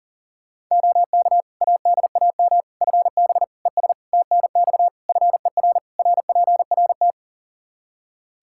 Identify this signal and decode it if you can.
Morse code, heard as a keyed tone, about 40 wpm, 700 Hz. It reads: OK ADAM FB ES TNX FER RPRT
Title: Practice QSO - Exchange 2 - Info - 40wpm